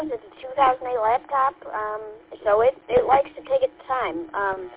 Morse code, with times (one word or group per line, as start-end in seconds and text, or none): none